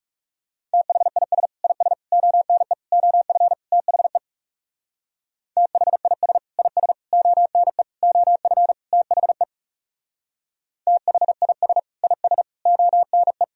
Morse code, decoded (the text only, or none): THIS IS ONE OF THE THIS IS ONE OF THE THIS IS ONE